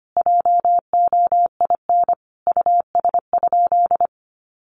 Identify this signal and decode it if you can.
Morse code, heard as a keyed tone, about 25 wpm, 700 Hz. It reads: JOIN US?
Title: Sentences from Top 1400 Words 25wpm